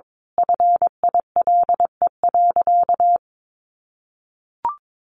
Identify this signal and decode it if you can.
Morse code, heard as a keyed tone, about 22 wpm, 700 Hz. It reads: FILE.
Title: Sentences from Top 1200 Words 22wpm